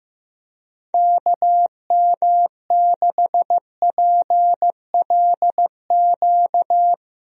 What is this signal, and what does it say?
Morse code, heard as a keyed tone, about 15 wpm, 700 Hz. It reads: KM6PLQ
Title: USA Call Signs - 6 Characters - Repeated 3x - 15wpm